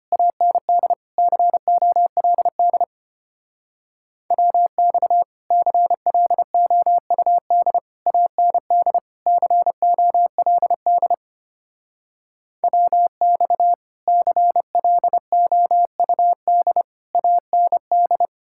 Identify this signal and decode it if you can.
Morse code, heard as a keyed tone, about 31 wpm, 700 Hz. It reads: AND COLD WX CLOUD AND COLD WX CLOUD AND
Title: Speed-Racing Common QSO Phrases Encore - 25wpm